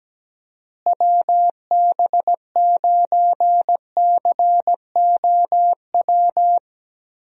Morse code, heard as a keyed tone, about 17 wpm, 700 Hz. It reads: WB9COW
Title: USA Call Signs - 6 Characters - Repeated 2x - 17wpm